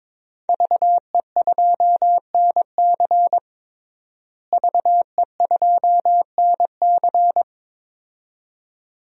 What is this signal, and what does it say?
Morse code, heard as a keyed tone, about 22 wpm, 700 Hz. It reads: VE2NC VE2NC
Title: DXCC-100 Call Signs - 5 Characters - Repeated 2x - 22wpm